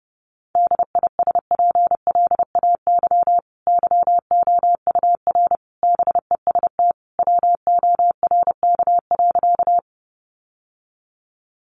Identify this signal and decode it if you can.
Morse code, heard as a keyed tone, about 28 wpm, 700 Hz. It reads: DISPLAY YOUR BEST WORK.
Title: Sentences from Top 1200 Words 30wpm